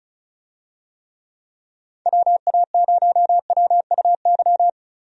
Morse code, heard as a keyed tone, about 34 wpm, 700 Hz. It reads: WA0WUY
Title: USA Call Signs - 6 Characters - Repeated 3x - 35wpm